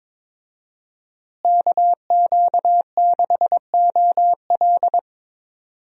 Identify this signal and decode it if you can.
Morse code, heard as a keyed tone, about 22 wpm, 700 Hz. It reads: KQ6OL
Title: USA Call Signs - 5 Characters - Repeated 3x - 22wpm